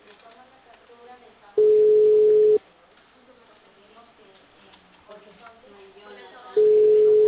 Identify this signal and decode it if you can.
Ringback tone (Europe, 425 Hz), with speech.